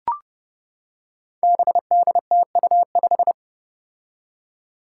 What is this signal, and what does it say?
Morse code, heard as a keyed tone, about 30 wpm, 700 Hz. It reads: BDTU5
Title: Five Challenging Letter-Number Combinations 30wpm